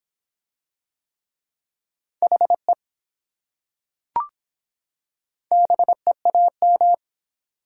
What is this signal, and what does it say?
Morse code, heard as a keyed tone, about 23 wpm, 700 Hz. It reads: HE BEAM